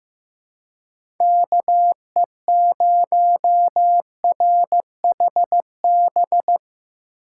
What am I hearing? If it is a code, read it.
Morse code, heard as a keyed tone, about 16 wpm, 700 Hz. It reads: KE0RHB
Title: USA Call Signs - 6 Characters - 4x Character Spacing - 15wpm